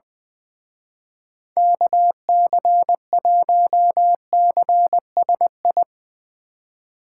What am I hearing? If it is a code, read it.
Morse code, heard as a keyed tone, about 21 wpm, 700 Hz. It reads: KC1CSI
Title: Speed Racing - USA 2x3 Call Signs 20wpm